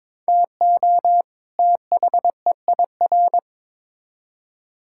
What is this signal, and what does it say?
Morse code, heard as a keyed tone, about 21 wpm, 700 Hz. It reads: TO THEIR
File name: Build A Phrase - 2-Word Phrases - 22wpm